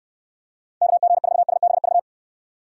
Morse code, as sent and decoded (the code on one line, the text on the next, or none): -.- --. ....- ... -... ...-
KG4SBV